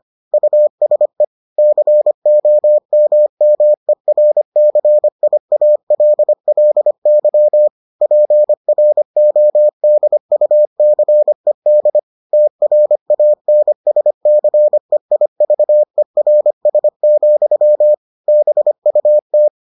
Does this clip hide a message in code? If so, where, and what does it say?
Yes: Morse code from 0.3 to 19.5 s, heard as a keyed tone, about 25 wpm, 600 Hz. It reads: USE COMMERCIALLY PRODUCED TRANSCEIVERS, BUT